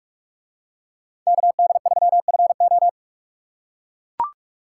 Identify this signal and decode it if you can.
Morse code, heard as a keyed tone, about 44 wpm, 700 Hz. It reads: KD3FO